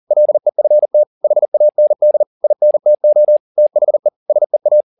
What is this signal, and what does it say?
Morse code, heard as a keyed tone, about 40 wpm, 600 Hz. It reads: LEFT HAND INTO THE SEA
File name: the old man and the sea 2 40wpm